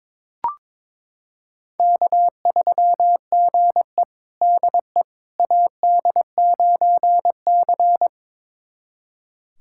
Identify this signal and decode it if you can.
Morse code, heard as a keyed tone, about 21 wpm, 700 Hz. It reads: K3GE DE AD9C
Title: USA Callsign DE Callsign - 4 Characters - 22wpm